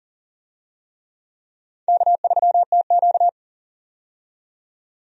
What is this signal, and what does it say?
Morse code, heard as a keyed tone, about 40 wpm, 700 Hz. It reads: K3TQ
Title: USA Call Signs - 4 Characters - 40wpm